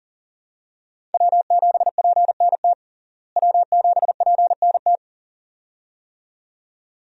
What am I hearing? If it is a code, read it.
Morse code, heard as a keyed tone, about 40 wpm, 700 Hz. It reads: W7PNT W7PNT
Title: USA Call Signs - 5 Characters - Repeated 2x - 40wpm